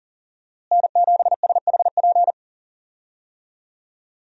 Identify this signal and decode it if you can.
Morse code, heard as a keyed tone, about 40 wpm, 700 Hz. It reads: N7SHP
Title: USA Call Signs - 5 Characters - Repeated 2x - 40wpm